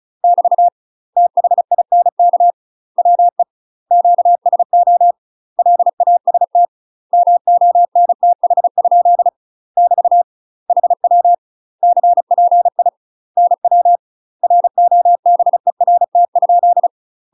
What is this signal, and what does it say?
Morse code, heard as a keyed tone, about 34 wpm, 700 Hz. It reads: <BT> THINK WE QSO LAST MONTH? <BT> HW CPI NW ROBERT?
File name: UK MASTER QSOs TRAINING / 35WPM / UK MASTER QSOs 0039